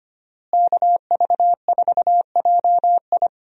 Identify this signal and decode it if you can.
Morse code, heard as a keyed tone, about 25 wpm, 700 Hz. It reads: KV4JI
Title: USA Call Signs - 5 Characters - Repeated 3x - 25wpm